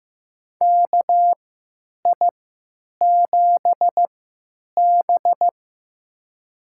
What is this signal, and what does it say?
Morse code, heard as a keyed tone, about 15 wpm, spaced as for 9 wpm, 700 Hz. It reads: KI7B
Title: USA Call Signs - 4 Characters - 3x Character Spacing - 15wpm